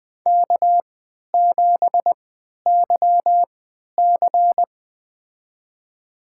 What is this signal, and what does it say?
Morse code, heard as a keyed tone, about 21 wpm, spaced as for 12 wpm, 700 Hz. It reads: K7YC
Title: USA Call Signs - 4 Characters - 3x Character Spacing - 20wpm